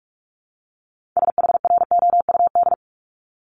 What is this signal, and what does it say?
Morse code, heard as a keyed tone, about 46 wpm, 700 Hz. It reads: SHROUD
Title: Multi-letter Phonemes 45wpm